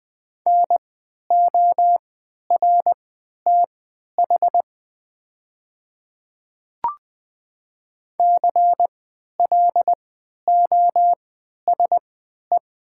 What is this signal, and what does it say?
Morse code, heard as a keyed tone, about 20 wpm, spaced as for 11 wpm, 700 Hz. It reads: NORTH CLOSE
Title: 3x Character Spacing - Top 300 Words - 20wpm